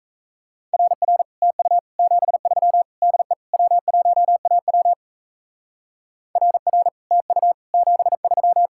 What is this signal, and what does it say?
Morse code, heard as a keyed tone, about 34 wpm, 700 Hz. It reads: RR TU 73 DE W1AW RR TU 73
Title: Speed-Racing Common QSO Phrases - 28wpm